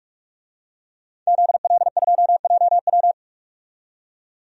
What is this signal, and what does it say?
Morse code, heard as a keyed tone, about 43 wpm, 700 Hz. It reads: ZL2JW
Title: DXCC-100 Call Signs - 5 Characters - 45wpm